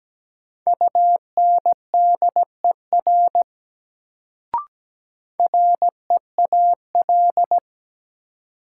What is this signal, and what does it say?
Morse code, heard as a keyed tone, about 17 wpm, 700 Hz. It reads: UNDER REAL
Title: Sentences from Top 300 Words - New Word Drill - 17wpm